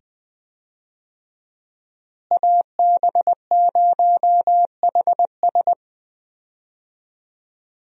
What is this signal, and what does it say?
Morse code, heard as a keyed tone, about 20 wpm, 700 Hz. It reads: AB0HS
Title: USA Call Signs - 5 Characters - Repeated 2x - 20wpm